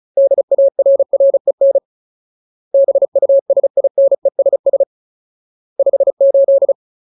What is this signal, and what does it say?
Morse code, heard as a keyed tone, about 34 wpm, 550 Hz. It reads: DARREN BUSINESS 58
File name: Ultimate Word List / 35WPM Ultimate